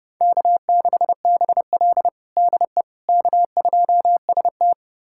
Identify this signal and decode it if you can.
Morse code, heard as a keyed tone, about 30 wpm, 700 Hz. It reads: K6BL DE K2ST
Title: USA Callsign DE Callsign - 4 Characters - 30wpm